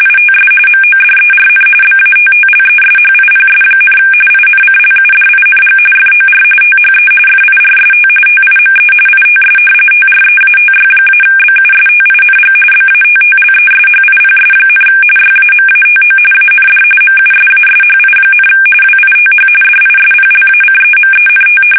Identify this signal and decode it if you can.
RTTY, about 45.45 baud. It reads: ITS MODERN BATTLESHIPS, INCLUDING FOR THE FIRST TIME ITS TREATY-BUSTING BATTLESHIPS, YAMATO AND MUSASHI, WITH DISPLACEMENT
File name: fsk850 / battleshipnj